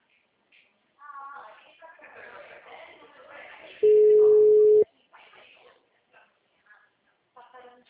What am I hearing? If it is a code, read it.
Ringback tone (Europe, 425 Hz), with speech.
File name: out- -2011- - - .57753